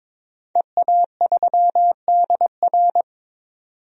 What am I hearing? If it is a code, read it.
Morse code, heard as a keyed tone, about 22 wpm, 700 Hz. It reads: EA3DR
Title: DXCC-100 Call Signs - 5 Characters - 22wpm